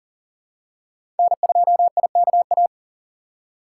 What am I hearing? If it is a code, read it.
Morse code, heard as a keyed tone, about 38 wpm, 700 Hz. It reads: N2IKA
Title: USA Call Signs - 5 Characters - 3x Character Spacing - 40wpm